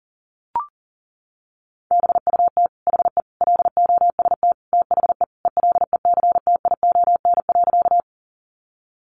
Morse code, heard as a keyed tone, about 40 wpm, 700 Hz. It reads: BUT HE LOST THE ELECTION.
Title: Sentences from Top 1500 Words 40wpm